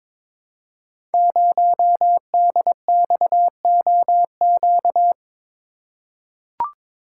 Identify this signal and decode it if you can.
Morse code, heard as a keyed tone, about 22 wpm, 700 Hz. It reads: 0DXOQ
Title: Code Groups - 5 Letter:Numbers - 22wpm